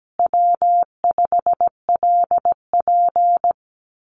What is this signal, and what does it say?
Morse code, heard as a keyed tone, about 17 wpm, 700 Hz. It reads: W5LP
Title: Common QSO Phrases Encore 17wpm